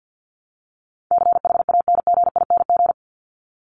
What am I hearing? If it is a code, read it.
Morse code, heard as a keyed tone, about 50 wpm, 700 Hz. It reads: CHANGING
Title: ING Suffix - Top 100 Words with Base Words 50wpm